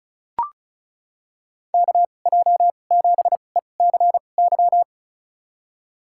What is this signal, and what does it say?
Morse code, heard as a keyed tone, about 36 wpm, spaced as for 25 wpm, 700 Hz. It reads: KJ7ECY
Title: USA Call Signs - 6 Characters - 2x Character Spacing - 35wpm